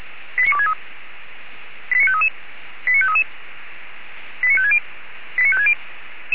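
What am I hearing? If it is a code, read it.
Five-tone selective call (ZVEI), heard as quick runs of five tones, tones of about 70 ms. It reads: 89263 89533 89533 89560 89560
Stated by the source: ZVEI five-tone call received by SDR.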